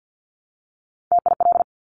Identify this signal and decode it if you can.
Morse code, heard as a keyed tone, about 50 wpm, 700 Hz. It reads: TIL
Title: Top Three Letter Combinations in Words 50wpm